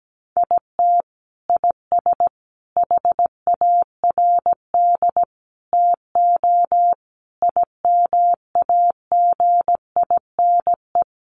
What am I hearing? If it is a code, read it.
Morse code, heard as a keyed tone, about 17 wpm, 700 Hz. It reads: IT IS HARD TO IMAGINE
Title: Sets of 5 Words Encore2 17wpm